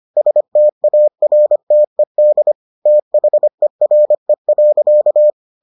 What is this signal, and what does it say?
Morse code, heard as a keyed tone, about 25 wpm, 600 Hz. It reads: STARTED THERE.